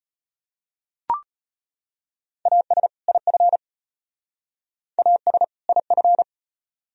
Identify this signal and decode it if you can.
Morse code, heard as a keyed tone, about 31 wpm, 700 Hz. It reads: AS IF AS IF
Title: Speed-Racing - Sets of 2 Words 25wpm